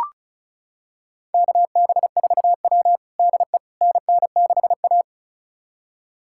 Morse code, heard as a keyed tone, about 34 wpm, 700 Hz. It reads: KB4W DE NN6A
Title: USA Callsign DE Callsign - 4 Characters - 35wpm